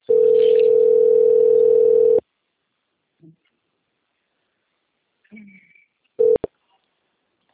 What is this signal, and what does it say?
Ringback tone (North America), 440 + 480 Hz.
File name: out- - - - - .20459